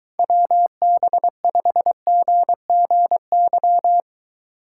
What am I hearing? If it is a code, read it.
Morse code, heard as a keyed tone, about 23 wpm, 700 Hz. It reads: WB5GGY